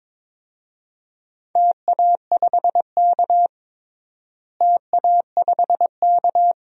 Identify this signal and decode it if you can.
Morse code, heard as a keyed tone, about 22 wpm, 700 Hz. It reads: TA5K TA5K
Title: DXCC-100 Call Signs - 4 Characters - Repeated 2x - 22wpm